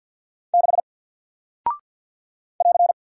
Morse code, heard as a keyed tone, about 48 wpm, 700 Hz. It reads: 6 <AR>
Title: Single Character (Letter:Number:Punctuation:Prosign) Alt 50wpm